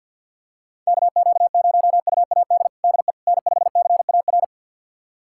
Morse code, heard as a keyed tone, about 52 wpm, 700 Hz. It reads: KQ0UAD DE N5CAR
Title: USA Callsign DE Callsign 50wpm